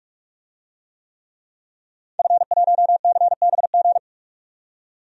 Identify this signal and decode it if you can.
Morse code, heard as a keyed tone, about 42 wpm, 700 Hz. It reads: F1CBG
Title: DXCC-100 Call Signs - 5 Characters - 45wpm